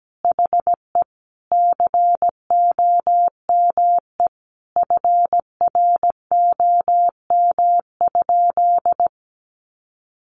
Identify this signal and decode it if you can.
Morse code, heard as a keyed tone, about 17 wpm, 700 Hz. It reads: HE COME FROM?
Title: Sentences from Top 300 Words 17wpm